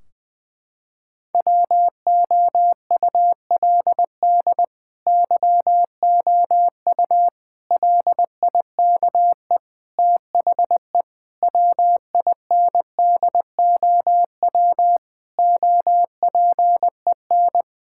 Morse code, heard as a keyed tone, about 20 wpm, 700 Hz. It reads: WOULD YOU LIKE THE WINDOW OPEN
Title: Sentences from Top 1600 Words 20wpm